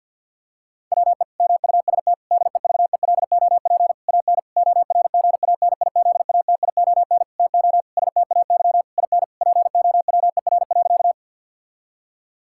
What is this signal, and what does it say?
Morse code, heard as a keyed tone, about 50 wpm, 700 Hz. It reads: WE MUST DEVELOP AN ORGANIZATION TO STAY IN POWER.
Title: Sentences from Top 1000 Words - Review Part 5 - 50wpm